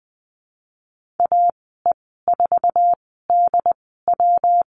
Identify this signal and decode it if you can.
Morse code, heard as a keyed tone, about 20 wpm, spaced as for 15 wpm, 700 Hz. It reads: AE4DW
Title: 154 Lesson 25g - Letter p - Call Signs 20wpm